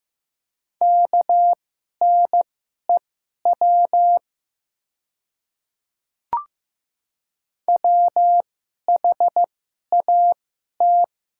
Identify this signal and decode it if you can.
Morse code, heard as a keyed tone, about 15 wpm, spaced as for 11 wpm, 700 Hz. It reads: KNEW WHAT
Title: 2x Character Spacing - Top 500 Words - 15wpm